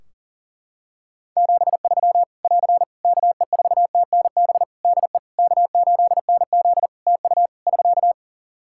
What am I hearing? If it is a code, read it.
Morse code, heard as a keyed tone, about 37 wpm, 700 Hz. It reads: 73 <AR> KE4TNB DE K8NZ TU <SK>
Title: Practice QSO - Exchange 4 - Ending - 40wpm